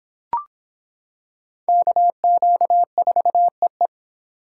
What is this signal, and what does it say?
Morse code, heard as a keyed tone, about 25 wpm, 700 Hz. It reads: KQ4EE